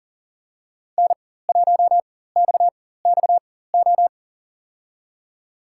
Morse code, heard as a keyed tone, about 40 wpm, spaced as for 19 wpm, 700 Hz. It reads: N1XXO
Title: USA Call Signs - 5 Characters - 4x Character Spacing - 40wpm